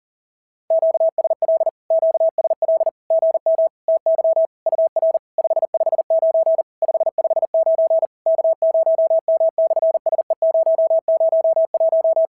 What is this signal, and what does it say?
Morse code, heard as a keyed tone, about 40 wpm, 650 Hz. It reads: QSL QSL GM TY UR 559 559 K0M/SE001
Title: SOTA QSO Practice 40wpm